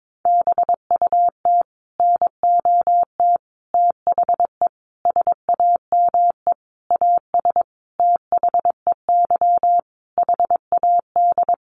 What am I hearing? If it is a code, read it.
Morse code, heard as a keyed tone, about 22 wpm, 700 Hz. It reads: BUT NOT THE SAME AS THEY HAD
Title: A Sound Of Thunder 22wpm